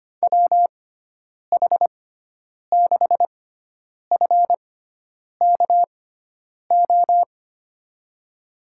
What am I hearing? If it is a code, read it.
Morse code, heard as a keyed tone, about 25 wpm, spaced as for 9 wpm, 700 Hz. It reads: WH6FKO
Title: USA Call Signs - 6 Characters - 6x Character Spacing - 25wpm